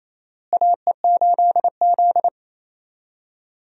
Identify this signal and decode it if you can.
Morse code, heard as a keyed tone, about 28 wpm, 700 Hz. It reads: AE8Z
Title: USA Call Signs - 4 Characters - Repeated 3x - 28wpm